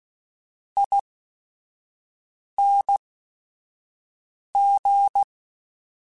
Morse code, heard as a keyed tone, about 16 wpm, spaced as for 5 wpm, 800 Hz. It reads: ING
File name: Short Runs / Short Runs-mp3s / ESE ING @5